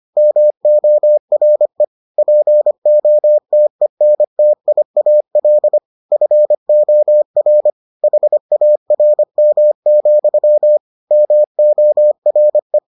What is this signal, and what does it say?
Morse code, heard as a keyed tone, about 25 wpm, 600 Hz. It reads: MORE POTENTIAL FOR HARM, MORE